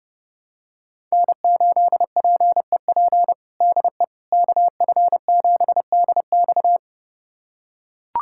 Morse code, heard as a keyed tone, about 29 wpm, 700 Hz. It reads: N8PEP DE KF7DX
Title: USA Callsign DE Callsign - 5 Characters - 30wpm